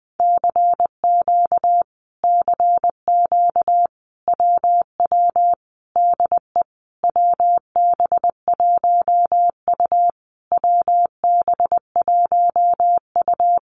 Morse code, heard as a keyed tone, about 19 wpm, 700 Hz. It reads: CQ CQ WW DE WB1U WB1U
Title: Common QSO Phrases Encore 20wpm